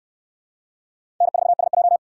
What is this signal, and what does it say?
Morse code, heard as a keyed tone, about 69 wpm, 700 Hz. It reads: N4SJ